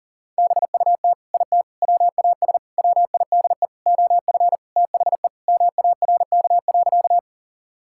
Morse code, heard as a keyed tone, about 37 wpm, 700 Hz. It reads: BUT IT WAS WIDE OF THE MARK.
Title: Sentences from Top 1000 Words - Review Part 6 - 40wpm